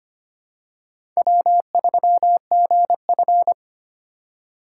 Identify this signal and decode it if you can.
Morse code, heard as a keyed tone, about 25 wpm, 700 Hz. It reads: W3GF
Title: USA Call Signs - 4 Characters - 4x Character Spacing - 25wpm